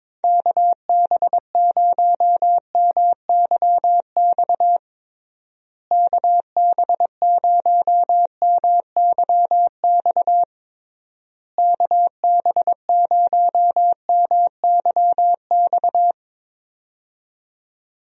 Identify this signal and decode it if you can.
Morse code, heard as a keyed tone, about 22 wpm, 700 Hz. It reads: KB0MYX KB0MYX KB0MYX
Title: USA Call Signs - 6 Characters - Repeated 3x - 22wpm